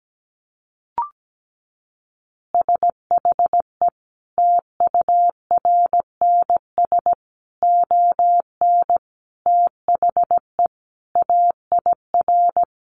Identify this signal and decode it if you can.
Morse code, heard as a keyed tone, about 17 wpm, 700 Hz. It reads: SHE TURNS ON THE AIR
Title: Additional Easy Phrases 17wpm